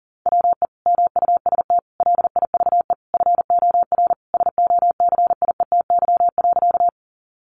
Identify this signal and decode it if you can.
Morse code, heard as a keyed tone, about 40 wpm, 700 Hz. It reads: WE MUST LIVE FOR SOCIETY.
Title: Sentences from Top 700 Words 40wpm